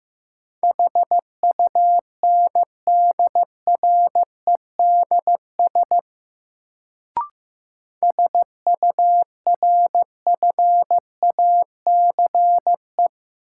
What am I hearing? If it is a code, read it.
Morse code, heard as a keyed tone, about 15 wpm, 700 Hz. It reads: HUNDREDS SURFACE
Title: Sets of 4 Words - New Word Drill - 15wpm